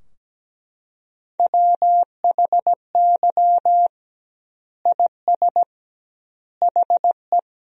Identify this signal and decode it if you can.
Morse code, heard as a keyed tone, about 17 wpm, 700 Hz. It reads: WHY IS HE
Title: 2x Word-Spacing - Sentences from Top 300 Words - 17wpm